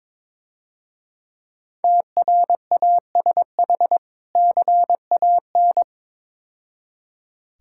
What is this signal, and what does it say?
Morse code, heard as a keyed tone, about 24 wpm, 700 Hz. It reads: TRASH CAN
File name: Easy Words - Speed-Racing with Hints - 22wpm